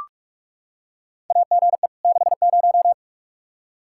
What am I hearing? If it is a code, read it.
Morse code, heard as a keyed tone, about 44 wpm, 700 Hz. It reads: AGE 60
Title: Practice QSO - Exchange 3 - More Info - 45wpm